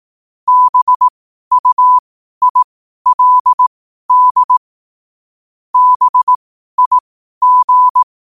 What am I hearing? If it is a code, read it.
Morse code, heard as a keyed tone, about 18 wpm, spaced as for 12 wpm, 1000 Hz. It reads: BUILD BIG
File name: qotdfiles / 12wpm